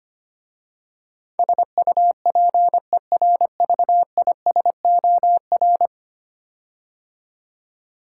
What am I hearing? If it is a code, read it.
Morse code, heard as a keyed tone, about 25 wpm, 690 Hz. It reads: SUPERVISOR
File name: 10 Letter Words - First 3 Letters Given - 25wpm